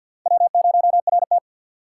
Morse code, heard as a keyed tone, about 51 wpm, 700 Hz. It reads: W0RT
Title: USA Call Signs - 4x Character Spacing - 50wpm